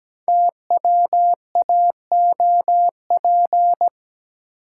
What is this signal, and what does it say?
Morse code, heard as a keyed tone, about 17 wpm, 700 Hz. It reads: TWAOP
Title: Code Groups - 5 Letters - 17wpm